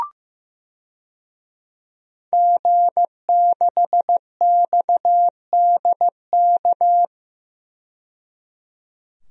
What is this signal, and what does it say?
Morse code, heard as a keyed tone, about 15 wpm, 700 Hz. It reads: G6XDK
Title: DXCC-100 Call Signs - 5 Characters - 15wpm